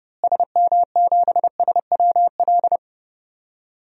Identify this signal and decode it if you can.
Morse code, heard as a keyed tone, about 30 wpm, 700 Hz. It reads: SM7SWL
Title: DXCC-100 Call Signs - 6 Characters - Repeated 2x - 30wpm